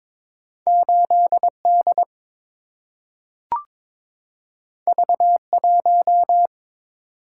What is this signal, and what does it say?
Morse code, heard as a keyed tone, about 22 wpm, 700 Hz. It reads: 8 D V 1
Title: Single-Letter Single-Number 22wpm